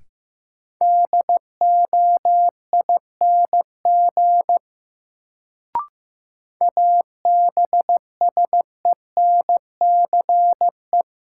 Morse code, heard as a keyed tone, about 15 wpm, 700 Hz. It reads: DOING ABSENCE
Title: Sets of 4 Words - New Word Drill - 15wpm